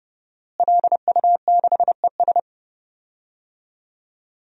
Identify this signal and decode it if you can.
Morse code, heard as a keyed tone, about 30 wpm, 700 Hz. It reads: LU6ES
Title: DXCC-100 Call Signs - 5 Characters - Repeated 2x - 30wpm